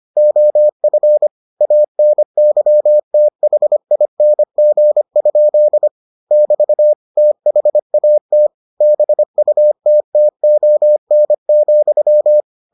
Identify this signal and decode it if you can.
Morse code, heard as a keyed tone, about 25 wpm, 600 Hz. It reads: OF ANYTHING? <BT> THAT BUTTON,